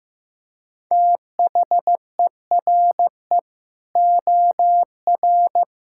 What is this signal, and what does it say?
Morse code, heard as a keyed tone, about 14 wpm, 700 Hz. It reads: THERE OR
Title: Sentences from Top 1600 Words 15wpm